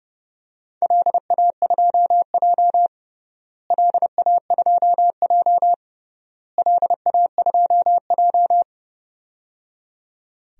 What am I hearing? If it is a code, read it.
Morse code, heard as a keyed tone, about 31 wpm, 700 Hz. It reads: LA2J LA2J LA2J
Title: DXCC-100 Call Signs - 4 Characters - Repeated 3x - 30wpm